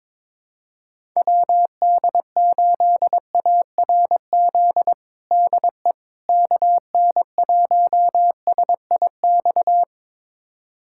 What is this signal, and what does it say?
Morse code, heard as a keyed tone, about 22 wpm, 700 Hz. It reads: WD8ARZ DE KN1SIX
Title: USA Callsign DE Callsign - 6 Characters - 22wpm